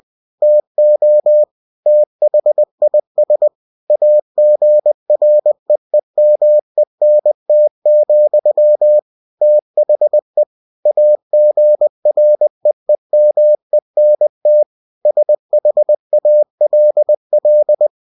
Morse code, heard as a keyed tone, about 20 wpm, 600 Hz. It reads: TO THIS AGREEMENT, THE AGREEMENT SHALL
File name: cw-books / winnie-the-pooh / winnie-20wpm-page-0119